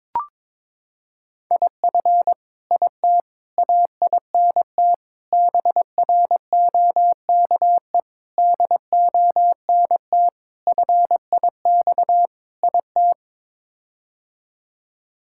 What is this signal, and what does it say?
Morse code, heard as a keyed tone, about 22 wpm, 700 Hz. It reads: IF IT AINT BROKE DONT FIX IT
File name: practice / 22wpm / Sets of 7 Words - 22wpm